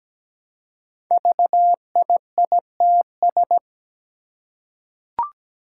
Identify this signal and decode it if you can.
Morse code, heard as a keyed tone, about 17 wpm, 700 Hz. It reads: VIITS
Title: Five Challenging Letter-Number Combinations 17wpm